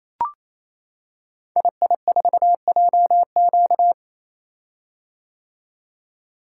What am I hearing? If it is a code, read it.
Morse code, heard as a keyed tone, about 27 wpm, 700 Hz. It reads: II4JQ
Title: Code Groups - 5 Letter:Numbers - 28wpm